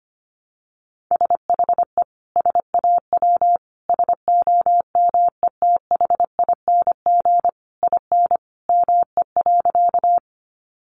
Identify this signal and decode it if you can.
Morse code, heard as a keyed tone, about 25 wpm, 700 Hz. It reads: SHE SAW SOMETHING IN ME.
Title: Sentences from Top 600 Words 25wpm